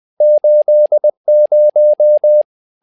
Morse code, heard as a keyed tone, about 20 wpm, 600 Hz. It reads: 80
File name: tower trouble 20wpm